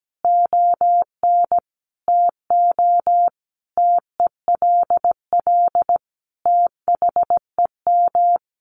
Morse code, heard as a keyed tone, about 17 wpm, 700 Hz. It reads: ON TO TELL THEM
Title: Sentences from Top 900 Words 17wpm